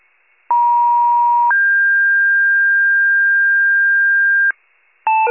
Two-tone page, heard as a one-second tone, then a three-second tone, about 950 then 1600 Hz, with noise. It is heard from 0.5 to 4.5 s.
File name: pager / 94 RESCUE DUTY / wcalert